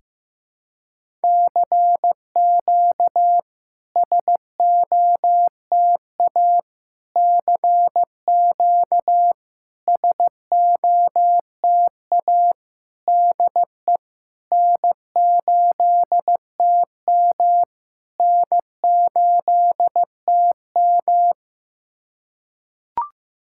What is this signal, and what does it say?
Morse code, heard as a keyed tone, about 15 wpm, 700 Hz. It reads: CQ SOTA CQ SOTA DE N8TM N8TM
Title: SOTA QSO Practice 15wpm